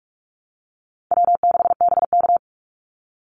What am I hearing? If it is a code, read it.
Morse code, heard as a keyed tone, about 44 wpm, 700 Hz. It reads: W6BK
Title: Call Signs Repeated 3x - 45wpm